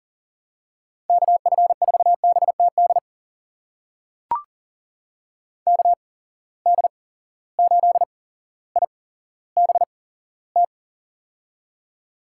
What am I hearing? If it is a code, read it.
Morse code, heard as a keyed tone, about 40 wpm, spaced as for 17 wpm, 700 Hz. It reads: KF4BTD KD8IBT
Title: USA Call Signs - 6 Characters - 8x Character Spacing - 40wpm